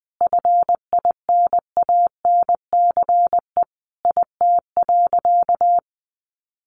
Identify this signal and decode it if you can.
Morse code, heard as a keyed tone, about 20 wpm, 700 Hz. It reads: FINANCE IT.
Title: Sentences from Top 1500 Words 20wpm